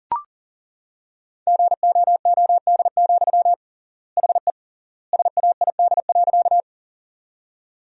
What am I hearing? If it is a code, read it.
Morse code, heard as a keyed tone, about 37 wpm, 700 Hz. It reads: GOOD, HE SAID.
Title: 3x Word-Spacing - Sentences from Top 200 Words - 40wpm